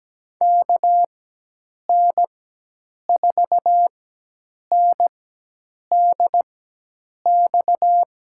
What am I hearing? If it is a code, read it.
Morse code, heard as a keyed tone, about 17 wpm, spaced as for 8 wpm, 700 Hz. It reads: KN4NDX
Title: USA Call Signs - 6 Characters - 4x Character Spacing - 17wpm